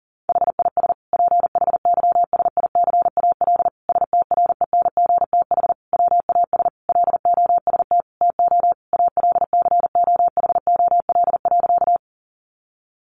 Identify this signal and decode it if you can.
Morse code, heard as a keyed tone, about 40 wpm, 700 Hz. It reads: HIS PHYSICAL STRENGTH WAS LOST TO ALCOHOL.
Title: Sentences from Top 1400 Words 40wpm